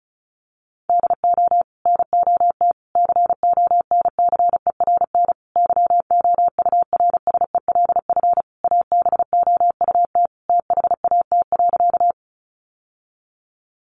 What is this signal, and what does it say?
Morse code, heard as a keyed tone, about 34 wpm, 700 Hz. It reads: DO NOT CONCERN YOURSELF ABOUT THAT.
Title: Sentences from Top 1000 Words 35wpm